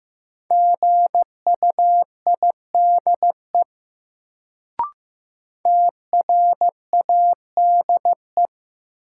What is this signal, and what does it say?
Morse code, heard as a keyed tone, about 15 wpm, 700 Hz. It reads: GUIDE TRADE
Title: Sentences from Top 600 Words - New Word Drill - 15wpm